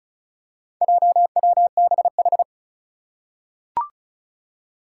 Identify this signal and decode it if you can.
Morse code, heard as a keyed tone, about 36 wpm, 690 Hz. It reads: JWBH
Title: Four Challenging Letter-Number Combinations 35wpm